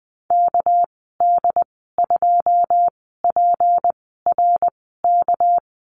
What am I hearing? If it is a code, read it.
Morse code, heard as a keyed tone, about 20 wpm, spaced as for 15 wpm, 700 Hz. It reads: KD2PRK
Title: 200 Lesson 32g - Letter k - Call Signs 20wpm